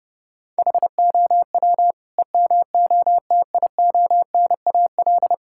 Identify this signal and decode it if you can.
Morse code, heard as a keyed tone, about 30 wpm, 700 Hz. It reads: HOW EMOTIONAL
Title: Sentences from Top 1000 Words - Review Part 5 - 30wpm